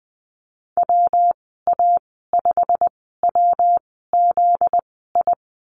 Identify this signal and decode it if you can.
Morse code, heard as a keyed tone, about 20 wpm, spaced as for 15 wpm, 700 Hz. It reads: WA5WZI
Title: 245 Lesson 39g - Letter z - Call Signs 20wpm